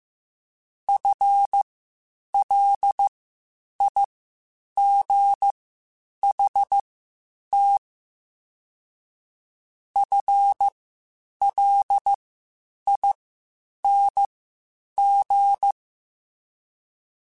Morse code, heard as a keyed tone, about 15 wpm, spaced as for 8 wpm, 800 Hz. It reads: FLIGHT FLING